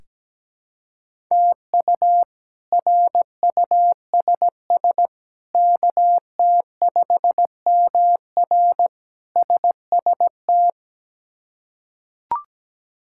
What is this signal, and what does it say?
Morse code, heard as a keyed tone, about 17 wpm, 700 Hz. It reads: TU RUSS KT5MR SST
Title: K1USN SST QSO Practice - 17wpm